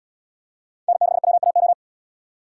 Morse code, heard as a keyed tone, about 73 wpm, 700 Hz. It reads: N6WAO